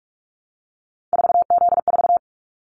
Morse code, heard as a keyed tone, about 44 wpm, 700 Hz. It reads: 4Z4
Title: Callsign Prefixes for Top 50 DXCC - 45wpm